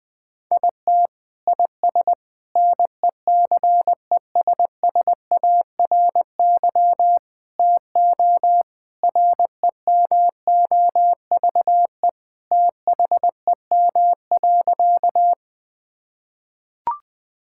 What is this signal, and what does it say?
Morse code, heard as a keyed tone, about 20 wpm, 700 Hz. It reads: IT IS NECESSARY TO REMOVE THEM.
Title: Sentences from Top 1000 Words - Review Part 5 - 20wpm